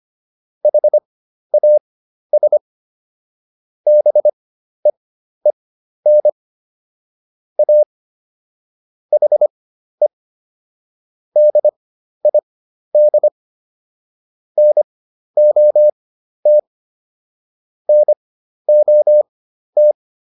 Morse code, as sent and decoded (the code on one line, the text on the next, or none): .... .- ... / -... . -. / .- / .... / -.. .. -.. / -. --- - / -. --- -
HAS BEEN A HE DID NOT NOT